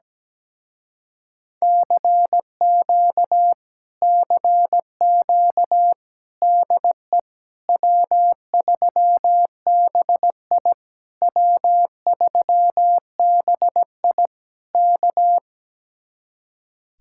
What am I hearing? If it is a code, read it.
Morse code, heard as a keyed tone, about 17 wpm, 700 Hz. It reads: CQ CQ DE W3BI W3BI K
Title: Speed-Racing Common QSO Phrases Encore - 17wpm